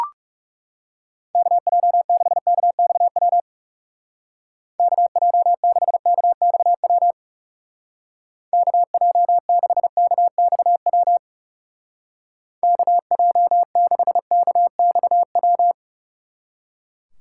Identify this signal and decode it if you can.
Morse code, heard as a keyed tone, about 36 wpm, 700 Hz. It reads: KJ6KXW KJ6KXW KJ6KXW KJ6KXW
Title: Speed Racing - USA 2x3 Call Signs 30wpm